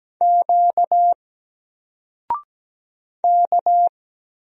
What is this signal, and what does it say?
Morse code, heard as a keyed tone, about 17 wpm, 700 Hz. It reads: Q K
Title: Single Letter-Number 17wpm